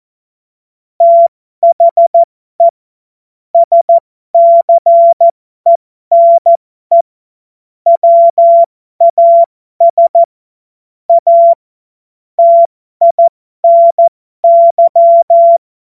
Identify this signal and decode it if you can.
Morse code, heard as a keyed tone, about 14 wpm, spaced as for 12 wpm, 680 Hz. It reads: THE SCENE WAS A TINY
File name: Morsekurs / Witze